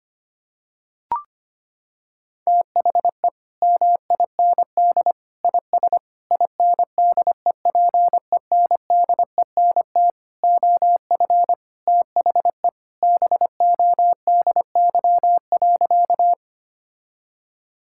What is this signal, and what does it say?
Morse code, heard as a keyed tone, about 25 wpm, 700 Hz. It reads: THE MIND IS INDEPENDENT OF THE BODY.
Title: Sentences from Top 1000 Words - Review Part 6 - 25wpm